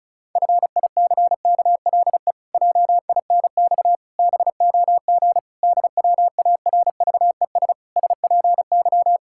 Morse code, heard as a keyed tone, about 36 wpm, 700 Hz. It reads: FICKLE JINX BOG DWARVES SPY
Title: Pangram Challenge v2 - 35wpm